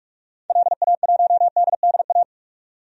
Morse code, heard as a keyed tone, about 42 wpm, 700 Hz. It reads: LA1DDA
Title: DXCC-100 Call Signs - 6 Characters - 45wpm